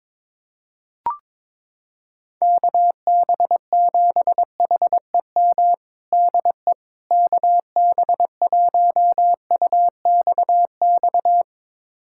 Morse code, heard as a keyed tone, about 22 wpm, 700 Hz. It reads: KB7HEM DE KB1UXX
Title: USA Callsign DE Callsign - 6 Characters - 22wpm